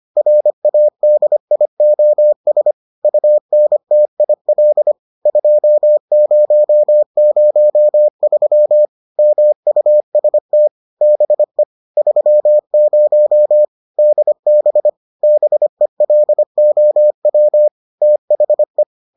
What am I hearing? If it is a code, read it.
Morse code, heard as a keyed tone, about 25 wpm, 600 Hz. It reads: RADIOS UNTIL 2003 MUST BE 30 DB BELOW THE